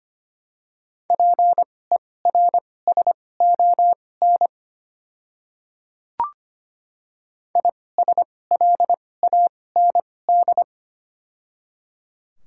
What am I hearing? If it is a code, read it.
Morse code, heard as a keyed tone, about 25 wpm, spaced as for 18 wpm, 700 Hz. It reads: PERSON ISLAND
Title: 2x Character Spacing - Top 500 Words - 25wpm